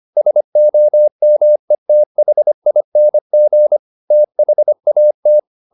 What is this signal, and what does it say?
Morse code, heard as a keyed tone, about 25 wpm, 600 Hz. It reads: SOMETHING THAT
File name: the old man and the sea part 1 25wpm